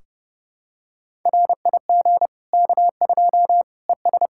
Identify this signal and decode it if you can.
Morse code, heard as a keyed tone, about 30 wpm, 700 Hz. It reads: RIG K2 ES
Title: Practice QSO - Exchange 2 - Info - 30wpm